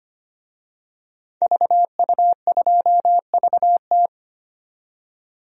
Morse code, heard as a keyed tone, about 25 wpm, 700 Hz. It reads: VU2VT
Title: DXCC-100 Call Signs - 5 Characters - 25wpm